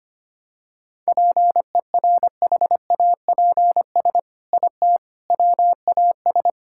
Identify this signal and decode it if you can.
Morse code, heard as a keyed tone, about 25 wpm, 700 Hz. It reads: PERHAPS IT WAS
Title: Sentences from Top 1600 Words 25wpm